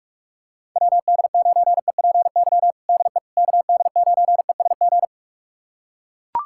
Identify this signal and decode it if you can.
Morse code, heard as a keyed tone, about 41 wpm, 700 Hz. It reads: WD9EPY DE KD9ESG
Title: USA Callsign DE Callsign 45wpm